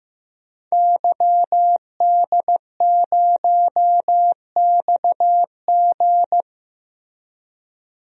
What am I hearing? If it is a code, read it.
Morse code, heard as a keyed tone, about 15 wpm, 700 Hz. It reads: YD0XG
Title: DXCC-100 Call Signs - 5 Characters - Repeated 2x - 15wpm